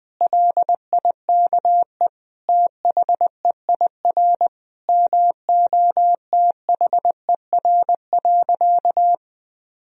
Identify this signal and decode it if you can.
Morse code, heard as a keyed tone, about 21 wpm, 700 Hz. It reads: LIKE THEIR MOTHER.
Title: Sentences from Top 1000 Words - Review Part 3 - 20wpm